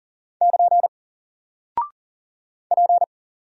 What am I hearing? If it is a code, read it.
Morse code, heard as a keyed tone, about 40 wpm, 700 Hz. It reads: <KN> P
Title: Single Character (Letter:Number:Punctuation:Prosign) Alt 40wpm